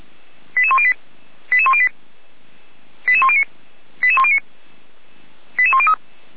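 Five-tone selective call (ZVEI), heard as quick runs of five tones, tones of about 70 ms. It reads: 88108 88108 88109 88109 88103
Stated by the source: ZVEI five-tone selective call 88102, received by SDR, recorded by the Orlamünde fire brigade (feuerwehr-orlamuende).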